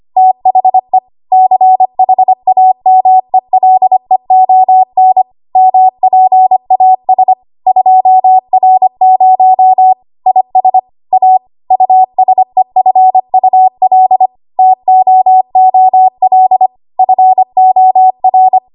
Morse code, heard as a keyed tone, about 25 wpm, 750 Hz. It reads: THE CHAMELEON MPAS 2R0 IS A USEFUL TOOL FOR
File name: Morse / 25WPM